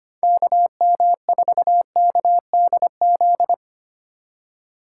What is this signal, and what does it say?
Morse code, heard as a keyed tone, about 31 wpm, 700 Hz. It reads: KM4KDZ
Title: Speed Racing - USA 2x3 Call Signs 25wpm